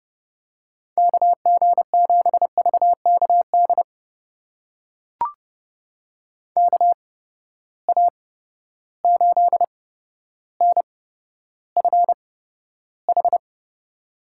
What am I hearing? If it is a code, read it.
Morse code, heard as a keyed tone, about 30 wpm, spaced as for 13 wpm, 700 Hz. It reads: KG7VKD KA8NFH
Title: USA Call Signs - 6 Characters - 8x Character Spacing - 30wpm